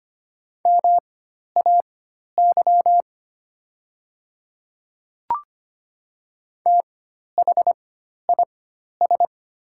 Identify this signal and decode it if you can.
Morse code, heard as a keyed tone, about 25 wpm, spaced as for 12 wpm, 700 Hz. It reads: MAY THIS
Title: 4x Character Spacing - Top 100 Words - 25wpm